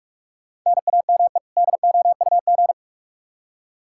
Morse code, heard as a keyed tone, about 47 wpm, 700 Hz. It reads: NAME DOUG
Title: Practice QSO - Exchange 1 - Beginning - 45wpm